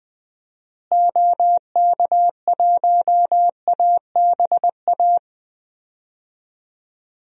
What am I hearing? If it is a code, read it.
Morse code, heard as a keyed tone, about 21 wpm, 700 Hz. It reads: OK1ABA
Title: DXCC-100 Call Signs - 6 Characters - Repeated 3x - 20wpm